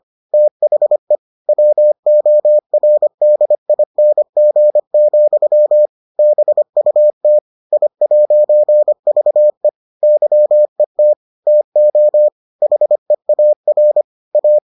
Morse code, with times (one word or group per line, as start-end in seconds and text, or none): 0.3-1.2 THE
1.5-5.9 WORDING,
6.2-7.4 BUT
7.7-9.7 I'VE
10.0-11.1 YET
11.5-12.3 TO
12.6-14.0 HEAR
14.3-14.6 A